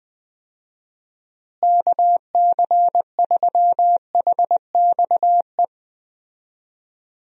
Morse code, heard as a keyed tone, about 21 wpm, 700 Hz. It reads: KC3HXE
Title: USA Call Signs - 6 Characters - Repeated 3x - 20wpm